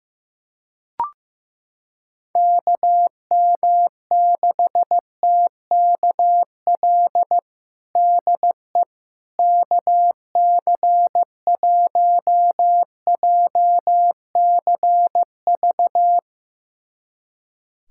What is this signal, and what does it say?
Morse code, heard as a keyed tone, about 15 wpm, 700 Hz. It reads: KM6TKL DE KC1JCV
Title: USA Callsign DE Callsign 15wpm